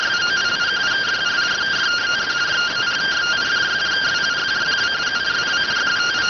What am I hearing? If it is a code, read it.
Two-tone teleprinter signal, about 170 Hz shift.